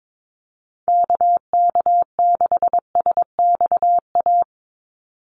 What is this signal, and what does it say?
Morse code, heard as a keyed tone, about 22 wpm, 700 Hz. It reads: KK6SXA
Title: USA Call Signs 22wpm